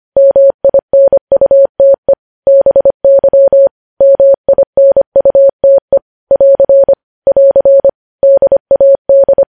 Morse code, heard as a keyed tone, about 25 wpm, 550 Hz. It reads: MINUTE BY MINUTE <AR> <AR> DAD
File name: podcast / News headlines-1x-25x25